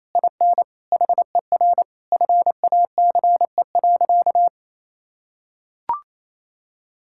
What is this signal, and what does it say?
Morse code, heard as a keyed tone, about 28 wpm, 700 Hz. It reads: IN HER FACE.
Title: Sentences from Top 1600 Words 28wpm